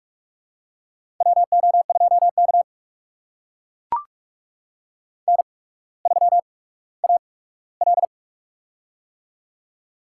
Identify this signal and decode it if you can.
Morse code, heard as a keyed tone, about 44 wpm, spaced as for 19 wpm, 700 Hz. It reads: WO2K N3AL
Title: USA Call Signs - 4 Characters - 8x Character Spacing - 45wpm